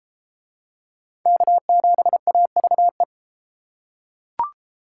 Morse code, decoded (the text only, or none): K7AVE